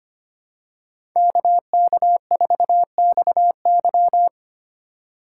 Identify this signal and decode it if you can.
Morse code, heard as a keyed tone, about 25 wpm, 700 Hz. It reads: KK4XY
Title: USA Call Signs - 5 Characters - 2x Character Spacing - 25wpm